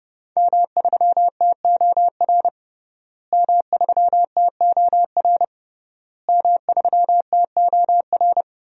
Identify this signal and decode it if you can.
Morse code, heard as a keyed tone, about 29 wpm, 700 Hz. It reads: M3TOR M3TOR M3TOR
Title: DXCC-100 Call Signs - 5 Characters - Repeated 3x - 30wpm